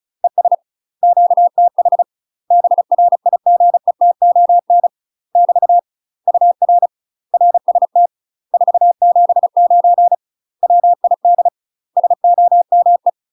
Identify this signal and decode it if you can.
Morse code, heard as a keyed tone, about 34 wpm, 700 Hz. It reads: ES QTH BRIGETON <BT> UR RST 479 WID SOME
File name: UK MASTER QSOs TRAINING / 35WPM / UK MASTER QSOs 0001